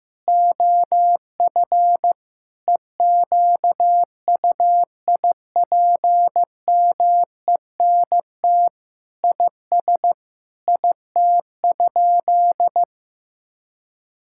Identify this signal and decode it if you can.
Morse code, heard as a keyed tone, about 15 wpm, 700 Hz. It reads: OF EQUIPMENT IS IT?
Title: Sentences from Top 1000 Words - Review Part 5 - 15wpm